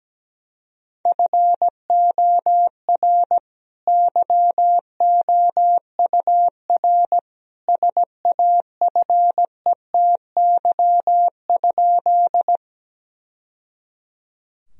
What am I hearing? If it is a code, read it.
Morse code, heard as a keyed tone, about 17 wpm, 700 Hz. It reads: FOR YOUR SAFETY?
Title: Sentences from Top 1000 Words - Review Part 4 - 17wpm